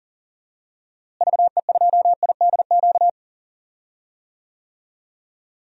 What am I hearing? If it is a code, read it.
Morse code, heard as a keyed tone, about 39 wpm, 700 Hz. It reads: VE2IDQ
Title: DXCC-100 Call Signs - 6 Characters - Repeated 2x - 40wpm